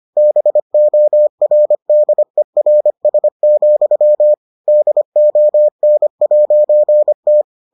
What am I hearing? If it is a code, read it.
Morse code, heard as a keyed tone, about 25 wpm, 600 Hz. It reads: BORDERS, DON'T